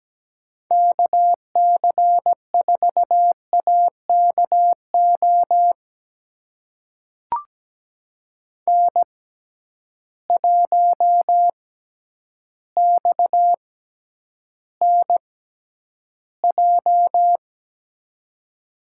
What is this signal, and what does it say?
Morse code, heard as a keyed tone, about 17 wpm, spaced as for 9 wpm, 700 Hz. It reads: KC4AKO N1XNJ
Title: USA Call Signs - 6x Character Spacing - 17wpm